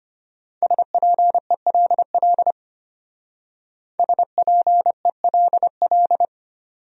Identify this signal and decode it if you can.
Morse code, heard as a keyed tone, about 30 wpm, 700 Hz. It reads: SPELL SPELL
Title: Speed-Racing Top 300 Words 25wpm